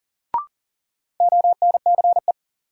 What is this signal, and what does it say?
Morse code, heard as a keyed tone, about 40 wpm, 700 Hz. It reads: ONCE
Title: Sentences from Top 300 Words - New Word Drill - 40wpm